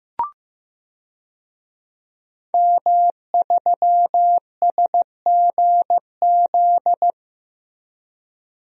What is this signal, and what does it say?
Morse code, heard as a keyed tone, about 15 wpm, 700 Hz. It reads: M3SGZ
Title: DXCC-100 Call Signs - 5 Characters - Repeated 2x - 15wpm